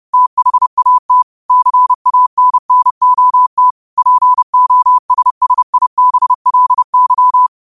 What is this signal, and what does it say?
Morse code, heard as a keyed tone, about 30 wpm, 1000 Hz. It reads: THAT CANNOT POSSIBLY
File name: qotdfiles / 30wpm